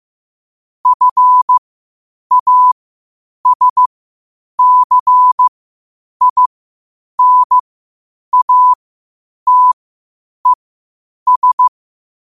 Morse code, heard as a keyed tone, about 15 wpm, spaced as for 8 wpm, 1000 Hz. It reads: FASCINATES
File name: qotdfiles / 8wpm